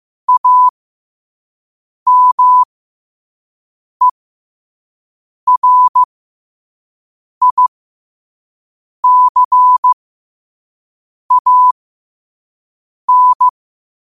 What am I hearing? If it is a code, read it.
Morse code, heard as a keyed tone, about 15 wpm, spaced as for 5 wpm, 1000 Hz. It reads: AMERICAN
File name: qotdfiles / 5wpm